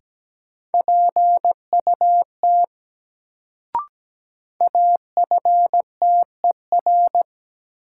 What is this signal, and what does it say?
Morse code, heard as a keyed tone, about 17 wpm, 700 Hz. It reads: PUT AFTER
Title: Sentences from Top 200 Words - New Word Drill - 17wpm